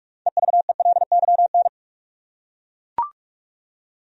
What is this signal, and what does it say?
Morse code, heard as a keyed tone, about 41 wpm, 700 Hz. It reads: EVELYN
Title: USA Common Names - 45wpm